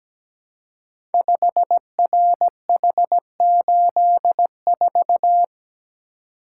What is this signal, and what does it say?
Morse code, heard as a keyed tone, about 17 wpm, 700 Hz. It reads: 5RH84
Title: Code Groups - 5 Letter:Numbers - 17wpm